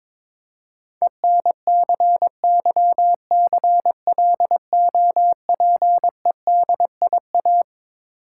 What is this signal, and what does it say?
Morse code, heard as a keyed tone, about 22 wpm, 690 Hz. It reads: ENCYCLOPEDIA
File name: Difficult Words with Hints - 22wpm